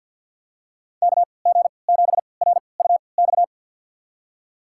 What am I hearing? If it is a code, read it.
Morse code, heard as a keyed tone, about 50 wpm, spaced as for 28 wpm, 700 Hz. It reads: KG7RUX
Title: USA Call Signs - 6 Characters - 3x Character Spacing - 50wpm